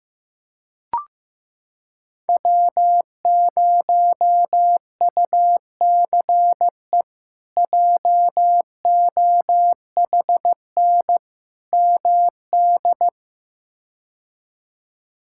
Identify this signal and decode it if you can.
Morse code, heard as a keyed tone, about 15 wpm, 700 Hz. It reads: W0UCE JOHN MD
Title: K1USN SST QSO Practice - 15wpm